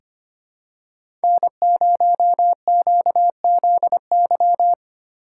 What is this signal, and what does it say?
Morse code, heard as a keyed tone, about 25 wpm, 700 Hz. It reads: N0QZY
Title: USA Call Signs - 5 Characters - 4x Character Spacing - 25wpm